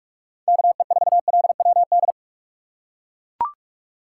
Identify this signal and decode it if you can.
Morse code, heard as a keyed tone, about 42 wpm, 700 Hz. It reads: KE4LWD
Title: USA Call Signs - 6 Characters - 2x Character Spacing - 45wpm